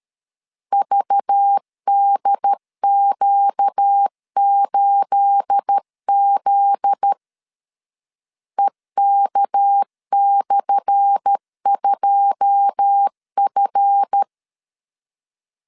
Morse code, heard as a keyed tone, about 13 wpm, 800 Hz. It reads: VDQ8Z EK/2F
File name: LEZIONI / 10 SIGNS / 10A SLASH 12wpm 2314